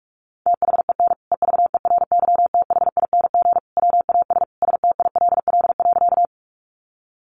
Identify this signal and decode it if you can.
Morse code, heard as a keyed tone, about 44 wpm, 700 Hz. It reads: THEN EVERYTHING WAS STILL.
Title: Sentences from Top 700 Words 45wpm